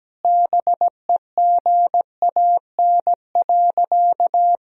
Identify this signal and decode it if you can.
Morse code, heard as a keyed tone, about 17 wpm, 700 Hz. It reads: BEGAN.
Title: Sentences from Top 1000 Words - Review Part 1 - 17wpm